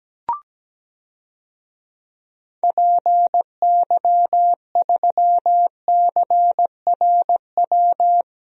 Morse code, heard as a keyed tone, about 17 wpm, 700 Hz. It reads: PY3CRW
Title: DXCC-100 Call Signs - 6 Characters - 17wpm